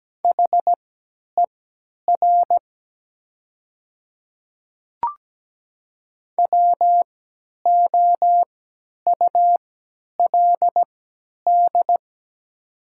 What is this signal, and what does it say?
Morse code, heard as a keyed tone, about 17 wpm, spaced as for 10 wpm, 700 Hz. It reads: HER WOULD
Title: 3x Character Spacing - Top 300 Words - 17wpm